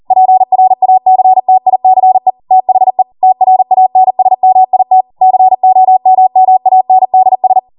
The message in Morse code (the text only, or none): PRACTICE THE TRANSMIT COMMANDS